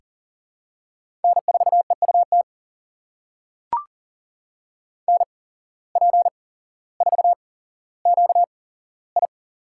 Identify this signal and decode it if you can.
Morse code, heard as a keyed tone, about 40 wpm, spaced as for 17 wpm, 700 Hz. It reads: N4EUT NP4QI
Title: USA Call Signs - 5 Characters - 8x Character Spacing - 40wpm